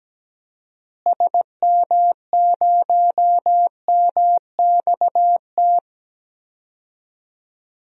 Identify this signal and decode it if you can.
Morse code, heard as a keyed tone, about 17 wpm, 700 Hz. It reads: SM0MXT
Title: DXCC-100 Call Signs - 6 Characters - Repeated 3x - 17wpm